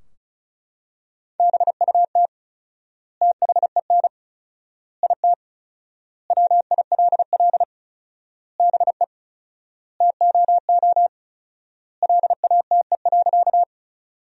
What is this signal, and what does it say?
Morse code, heard as a keyed tone, about 34 wpm, 700 Hz. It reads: BUT THEN IT WILL BE TOO LATE.
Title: 4x Word-Spacing - Sentences from Top 300 Words - 35wpm